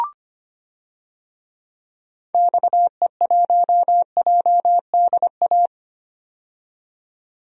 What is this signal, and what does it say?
Morse code, heard as a keyed tone, about 25 wpm, 700 Hz. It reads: XE1JDA
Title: DXCC-100 Call Signs - 6 Characters - 25wpm